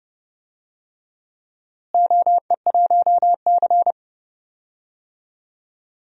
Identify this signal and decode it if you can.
Morse code, heard as a keyed tone, about 32 wpm, 700 Hz. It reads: OE1C
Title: DXCC-100 Call Signs - 4 Characters - Repeated 3x - 30wpm